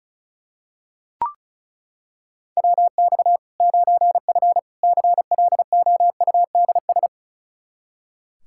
Morse code, heard as a keyed tone, about 36 wpm, 700 Hz. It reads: WX 9F CLOUDS
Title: K1USN SST QSO Practice 35wpm